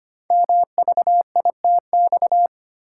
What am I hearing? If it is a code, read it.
Morse code, heard as a keyed tone, about 25 wpm, 700 Hz. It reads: MVITX
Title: Code Groups - 5 Letters - 25wpm